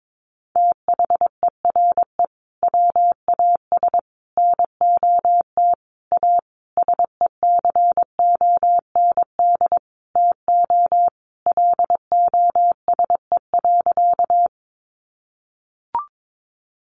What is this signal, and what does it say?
Morse code, heard as a keyed tone, about 22 wpm, 700 Hz. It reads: THERE WAS NOT A SECOND TO LOSE.
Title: Sentences from Top 1100 Words 22wpm